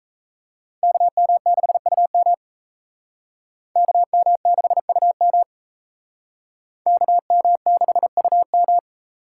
Morse code, heard as a keyed tone, about 37 wpm, 700 Hz. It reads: KM6UM KM6UM KM6UM